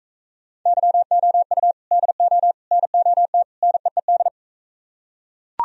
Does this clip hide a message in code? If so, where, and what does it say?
Yes: Morse code from 0.7 to 4.3 s, heard as a keyed tone, about 41 wpm, 700 Hz. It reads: YOU DO NOT NEED